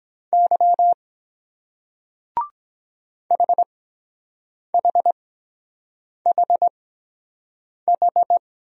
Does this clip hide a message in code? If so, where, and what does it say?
Yes: Morse code from 0.3 to 8.4 s, heard as a keyed tone, about 23 wpm, spaced as for 7 wpm, 700 Hz. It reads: Y HHHH